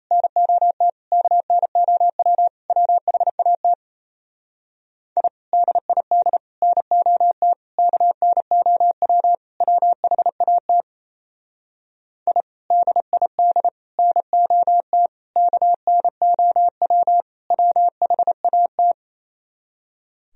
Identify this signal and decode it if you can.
Morse code, heard as a keyed tone, about 34 wpm, 700 Hz. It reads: NOT KNOW WHAT I DID NOT KNOW WHAT I DID NOT KNOW WHAT
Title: Speed-Racing - Sets of 5 Words 28wpm